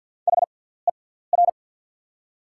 Morse code, heard as a keyed tone, about 48 wpm, spaced as for 17 wpm, 700 Hz. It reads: HER
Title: 6x Character Spacing - Top 500 Words - 50wpm